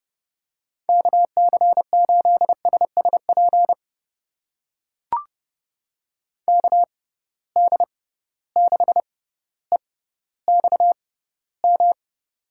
Morse code, heard as a keyed tone, about 32 wpm, spaced as for 16 wpm, 700 Hz. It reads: KC8SSP KD6EXM
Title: USA Call Signs - 6 Characters - 6x Character Spacing - 30wpm